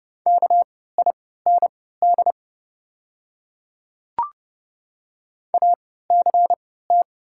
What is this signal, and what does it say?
Morse code, heard as a keyed tone, about 31 wpm, spaced as for 17 wpm, 700 Hz. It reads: KIND ACT
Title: 3x Character Spacing - Top 300 Words - 30wpm